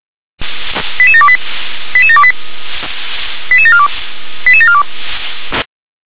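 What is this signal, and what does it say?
Five-tone selective call (ZVEI), heard as quick runs of five tones, tones of about 70 ms. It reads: 88728 88728 88742 88742
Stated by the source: ZVEI five-tone call received by SDR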